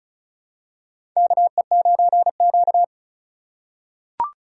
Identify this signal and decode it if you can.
Morse code, heard as a keyed tone, about 34 wpm, 700 Hz. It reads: KE9Q
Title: USA Call Signs - 4 Characters - 4x Character Spacing - 35wpm